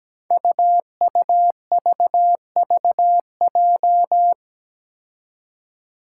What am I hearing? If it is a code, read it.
Morse code, heard as a keyed tone, about 17 wpm, 700 Hz. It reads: UUVVJ
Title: Five Challenging Letter-Number Combinations 17wpm